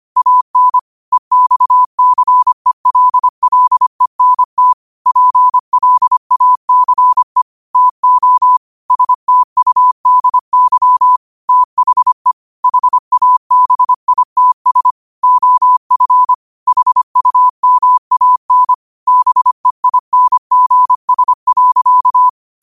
Morse code, heard as a keyed tone, about 25 wpm, 1000 Hz. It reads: AN EXCELLENT PLACE TO STUDY THE HABITS OF HUMAN BEINGS.
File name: qotdfiles / 25wpm